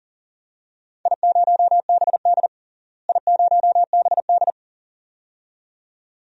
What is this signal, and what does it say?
Morse code, heard as a keyed tone, about 38 wpm, 700 Hz. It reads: I0BD I0BD
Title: DXCC-100 Call Signs - 4 Characters - Repeated 2x - 40wpm